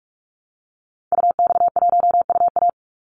Morse code, heard as a keyed tone, about 44 wpm, 700 Hz. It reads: UX1UA
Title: Call Signs - 45wpm